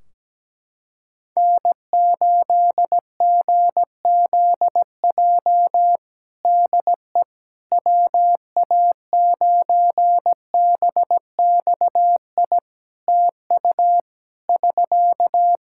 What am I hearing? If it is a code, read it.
Morse code, heard as a keyed tone, about 17 wpm, 700 Hz. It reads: N8GZJ DE WA9BXI TU <SK>
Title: K1USN SST QSO Practice 17wpm